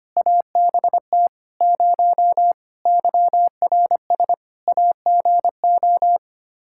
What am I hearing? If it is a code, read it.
Morse code, heard as a keyed tone, about 25 wpm, 700 Hz. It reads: ABT 0 YRS AGO
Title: Practice QSO - Exchange 3 - More Info - 25wpm